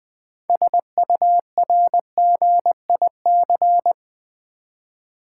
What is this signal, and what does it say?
Morse code, heard as a keyed tone, about 20 wpm, 700 Hz. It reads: SURGIC
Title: Build A Word - 3 Letters At A Time - 8-Letter Words - 20wpm